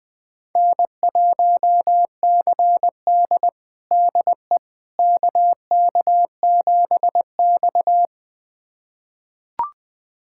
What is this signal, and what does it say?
Morse code, heard as a keyed tone, about 20 wpm, 700 Hz. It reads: N1CD DE KK7X
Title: USA Callsign DE Callsign - 4 Characters - 20wpm